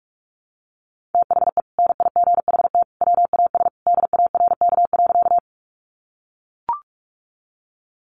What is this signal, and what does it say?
Morse code, heard as a keyed tone, about 44 wpm, 700 Hz. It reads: THE NIGHT WAS DARK.
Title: Sentences from Top 500 Words 45wpm